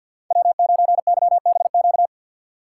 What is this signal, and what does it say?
Morse code, heard as a keyed tone, about 52 wpm, 700 Hz. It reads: W9YBQ
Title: USA Call Signs - 5 Characters - 4x Character Spacing - 50wpm